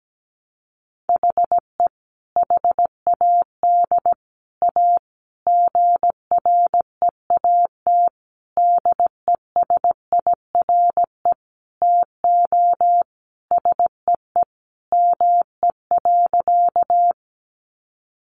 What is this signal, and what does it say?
Morse code, heard as a keyed tone, about 17 wpm, 700 Hz. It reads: HE HAD A GREAT DESIRE TO SEE ME.
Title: Sentences from Top 1100 Words 17wpm